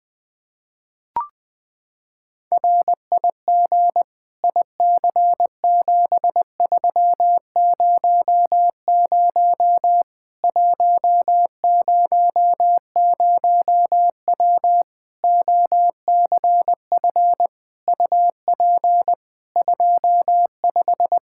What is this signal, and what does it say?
Morse code, heard as a keyed tone, about 20 wpm, 700 Hz. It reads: RIG IC7300 100W OCF UP 25
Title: Practice QSO - Exchange 2 - Info - 20wpm